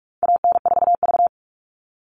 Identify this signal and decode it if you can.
Morse code, heard as a keyed tone, about 44 wpm, 700 Hz. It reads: AN4V
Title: All Cut Numbers - 4-Digit 45wpm